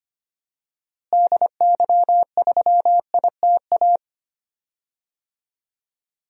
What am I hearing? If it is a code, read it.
Morse code, heard as a keyed tone, about 25 wpm, 700 Hz. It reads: DY3ITA
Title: DXCC-100 Call Signs - 6 Characters - Repeated 3x - 25wpm